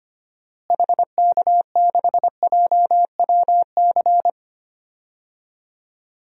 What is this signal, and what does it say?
Morse code, heard as a keyed tone, about 25 wpm, 700 Hz. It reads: HK6JWC
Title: DXCC-100 Call Signs - 6 Characters - Repeated 3x - 25wpm